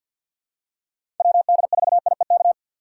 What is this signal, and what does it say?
Morse code, heard as a keyed tone, about 52 wpm, 690 Hz. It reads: WD4IEK
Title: USA Call Signs - 6 Characters - 3x Character Spacing - 50wpm